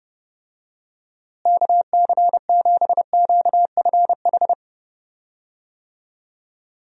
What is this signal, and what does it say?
Morse code, heard as a keyed tone, about 31 wpm, 700 Hz. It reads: KC7QFH
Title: USA Call Signs - 6 Characters - Repeated 3x - 30wpm